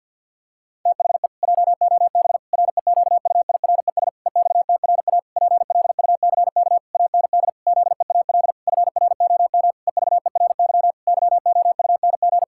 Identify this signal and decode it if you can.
Morse code, heard as a keyed tone, about 50 wpm, 700 Hz. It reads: THE JOB REQUIRES EXTRA PLUCK AND ZEAL FROM EVERY YOUNG
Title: Pangram Challenge v2 - 50wpm